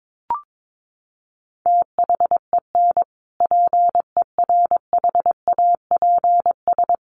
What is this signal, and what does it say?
Morse code, heard as a keyed tone, about 22 wpm, 700 Hz. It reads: THEN PERHAPS
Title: Sentences from Top 1100 Words 22wpm